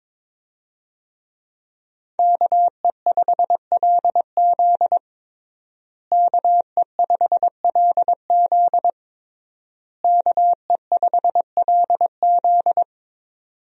Morse code, heard as a keyed tone, about 22 wpm, 700 Hz. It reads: KE5LZ KE5LZ KE5LZ
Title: USA Call Signs - 5 Characters - Repeated 3x - 22wpm